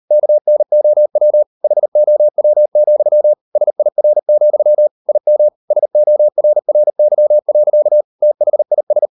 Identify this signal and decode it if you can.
Morse code, heard as a keyed tone, about 38 wpm, 600 Hz. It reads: KNOW HOW, SIR, IM SORRY. THIS